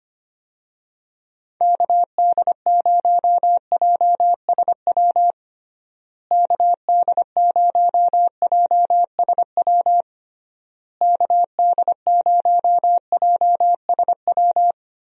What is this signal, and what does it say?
Morse code, heard as a keyed tone, about 25 wpm, 700 Hz. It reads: KD0JSW KD0JSW KD0JSW
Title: USA Call Signs - 6 Characters - Repeated 3x - 25wpm